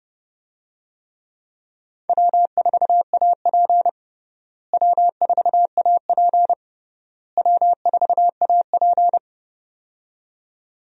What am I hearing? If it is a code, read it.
Morse code, heard as a keyed tone, about 30 wpm, 700 Hz. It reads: W4AP W4AP W4AP
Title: USA Call Signs - 4 Characters - Repeated 3x - 30wpm